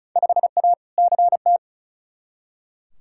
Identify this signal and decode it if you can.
Morse code, heard as a keyed tone, about 36 wpm, 700 Hz. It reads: 5A CT
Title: Field Day Exchanges - 35wpm